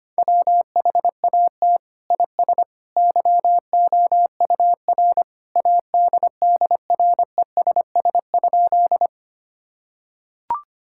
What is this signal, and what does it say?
Morse code, heard as a keyed tone, about 25 wpm, 700 Hz. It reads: WHAT IS YOUR ADDRESS?
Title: Sentences from Top 1000 Words - Review Part 6 - 25wpm